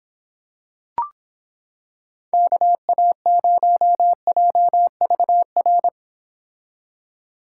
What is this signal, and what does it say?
Morse code, heard as a keyed tone, about 26 wpm, 700 Hz. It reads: KA0JVR